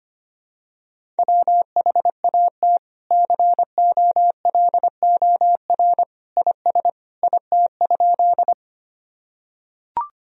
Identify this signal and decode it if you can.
Morse code, heard as a keyed tone, about 25 wpm, 700 Hz. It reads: WHAT COLOR IS IT?
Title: Sentences from Top 1000 Words - Review Part 2 - 25wpm